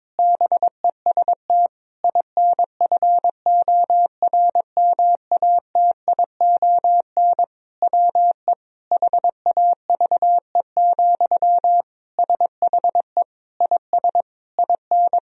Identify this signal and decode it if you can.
Morse code, heard as a keyed tone, about 22 wpm, 700 Hz. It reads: BEST INFORMATION WE HAVE, SHE IS IN
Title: Sentences from Top 1000 Words - Review Part 4 - 22wpm